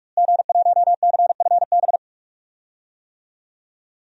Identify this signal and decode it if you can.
Morse code, heard as a keyed tone, about 42 wpm, 700 Hz. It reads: G1CFB
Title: DXCC-100 Call Signs - 5 Characters - Repeated 2x - 45wpm